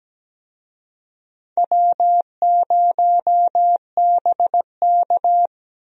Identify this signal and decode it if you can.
Morse code, heard as a keyed tone, about 17 wpm, 700 Hz. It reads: W0BK
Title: USA Call Signs - 4 Characters - Repeated 2x - 17wpm